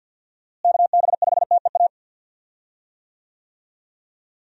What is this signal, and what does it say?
Morse code, heard as a keyed tone, about 53 wpm, 700 Hz. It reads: KB5TEA
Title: USA Call Signs - 6 Characters - Repeated 3x - 50wpm